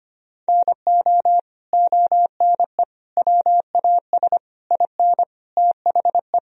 Morse code, heard as a keyed tone, about 25 wpm, 700 Hz. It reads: NO ONE WAS IN THE
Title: Sentences from Top 1000 Words - Review Part 4 - 25wpm